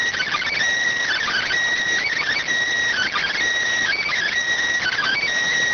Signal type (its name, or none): Emergency Alert System (SAME) tones